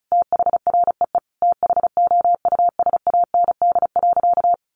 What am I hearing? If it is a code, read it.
Morse code, heard as a keyed tone, about 34 wpm, 700 Hz. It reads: THREE THOUSAND.
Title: Sentences from Top 1300 Words 35wpm